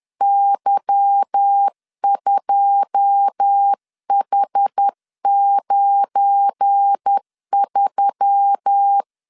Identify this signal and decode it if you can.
Morse code, heard as a keyed tone, about 10 wpm, 800 Hz. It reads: Y2H93
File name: CW / DET / 1 9 1500C S A 10wpm 4511